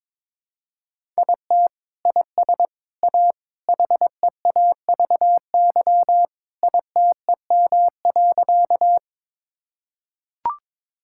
Morse code, heard as a keyed tone, about 22 wpm, 700 Hz. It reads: IT IS A HEAVY ITEM.
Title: Sentences from Top 1000 Words - Review Part 6 - 22wpm